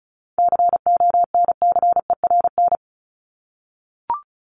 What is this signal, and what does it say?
Morse code, heard as a keyed tone, about 34 wpm, 700 Hz. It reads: CONCERN
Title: Top 751-1000 Words 35wpm